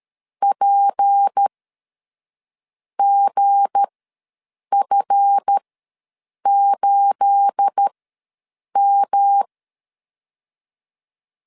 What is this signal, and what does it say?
Morse code, heard as a keyed tone, about 13 wpm, spaced as for 6 wpm, 800 Hz. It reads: PGF8M
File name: LEZIONI / 9 28 ws 12wpm 3354